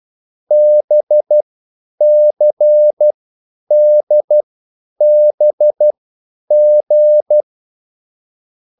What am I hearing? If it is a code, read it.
Morse code, heard as a keyed tone, about 12 wpm, spaced as for 9 wpm, 600 Hz. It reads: BCDBG